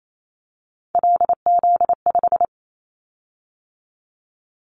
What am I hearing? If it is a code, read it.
Morse code, heard as a keyed tone, about 28 wpm, 700 Hz. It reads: LZ5
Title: Callsign Prefixes for Top 100 DXCC - 28wpm